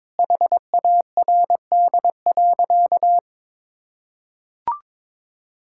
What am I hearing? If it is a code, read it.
Morse code, heard as a keyed tone, about 21 wpm, 700 Hz. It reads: HARD.
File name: practice / 22wpm / Sentences with 3-Word Phrases - 22wpm